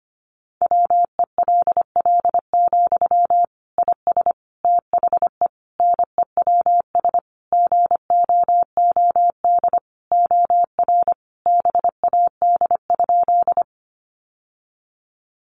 Morse code, heard as a keyed tone, about 25 wpm, 700 Hz. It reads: WELL, IS THE NEWS GOOD OR BAD?
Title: Sentences from Top 800 Words 25wpm